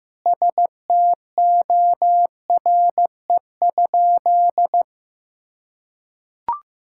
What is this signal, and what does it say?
Morse code, heard as a keyed tone, about 16 wpm, 700 Hz. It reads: STORE?
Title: Sentences from Top 1000 Words - Review Part 5 - 15wpm